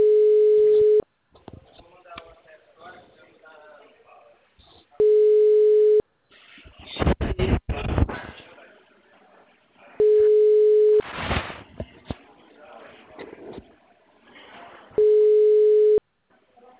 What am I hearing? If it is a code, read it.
Ringback tone (Europe, 425 Hz), with speech.
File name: out- -2010- - - .21